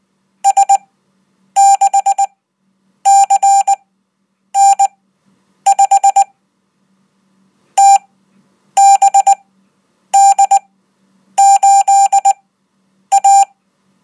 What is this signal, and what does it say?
Morse code, heard as a keyed tone, about 19 wpm, spaced as for 8 wpm, 790 Hz. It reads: S6CN5 TBD8A
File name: Code